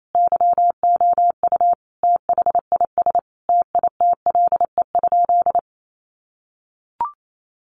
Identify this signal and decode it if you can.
Morse code, heard as a keyed tone, about 28 wpm, 700 Hz. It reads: YOU THIS TITLE?
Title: Sentences from Top 1200 Words 28wpm